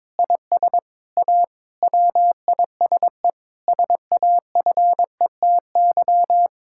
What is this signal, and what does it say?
Morse code, heard as a keyed tone, about 22 wpm, 700 Hz. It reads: IS A WISE SAFETY
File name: Difficult Sentences with Hints - 22wpm